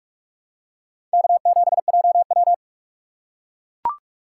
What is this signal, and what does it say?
Morse code, heard as a keyed tone, about 44 wpm, 700 Hz. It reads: K7JW
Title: USA Call Signs - 4 Characters - 3x Character Spacing - 45wpm